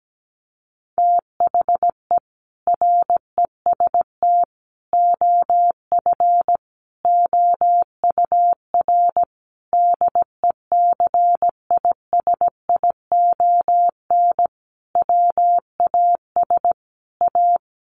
Morse code, heard as a keyed tone, about 17 wpm, 700 Hz. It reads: THE REST OF OUR DECISION WAS A
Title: Sentences from Top 1400 Words 17wpm